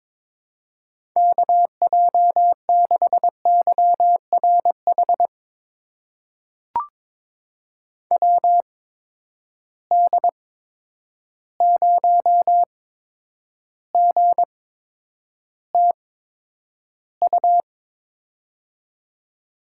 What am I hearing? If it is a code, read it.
Morse code, heard as a keyed tone, about 21 wpm, spaced as for 9 wpm, 700 Hz. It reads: KJ6YRH WD0GTU
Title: USA Call Signs - 6 Characters - 8x Character Spacing - 22wpm